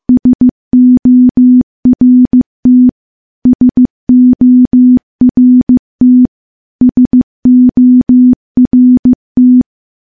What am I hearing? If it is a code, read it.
Morse code, heard as a keyed tone, about 15 wpm, 250 Hz. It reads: SORT SORT SORT
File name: morse / morse-training-15WPM